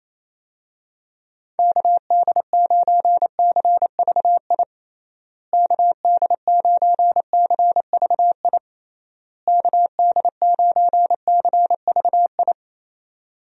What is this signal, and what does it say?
Morse code, heard as a keyed tone, about 28 wpm, 700 Hz. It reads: KD9CVI KD9CVI KD9CVI
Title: USA Call Signs - 6 Characters - Repeated 3x - 28wpm